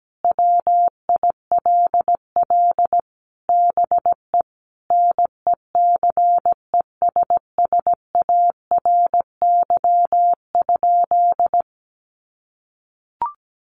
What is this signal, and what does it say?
Morse code, heard as a keyed tone, about 17 wpm, 700 Hz. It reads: WILL BE NECESSARY?
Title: Sentences from Top 800 Words 17wpm